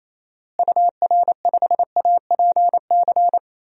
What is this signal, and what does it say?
Morse code, heard as a keyed tone, about 28 wpm, 700 Hz. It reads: UR5APC
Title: DXCC-100 Call Signs - 6 Characters - Repeated 2x - 28wpm